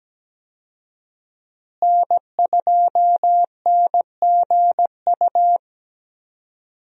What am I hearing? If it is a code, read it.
Morse code, heard as a keyed tone, about 17 wpm, 700 Hz. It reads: N2NGU
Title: USA Call Signs - 5 Characters - Repeated 3x - 17wpm